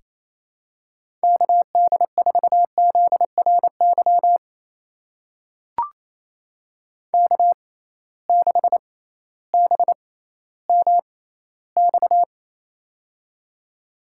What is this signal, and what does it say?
Morse code, heard as a keyed tone, about 28 wpm, spaced as for 15 wpm, 700 Hz. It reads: KD4ZRY K6BMX
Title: USA Call Signs - 6x Character Spacing - 28wpm